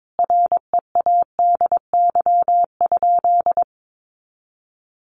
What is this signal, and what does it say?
Morse code, heard as a keyed tone, about 22 wpm, 700 Hz. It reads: READY?
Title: Sentences from Top 400 Words 22wpm